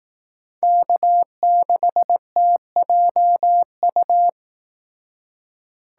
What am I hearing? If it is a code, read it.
Morse code, heard as a keyed tone, about 18 wpm, 700 Hz. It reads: K6TJU
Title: Speed Racing - USA 1x3 Call Signs 15wpm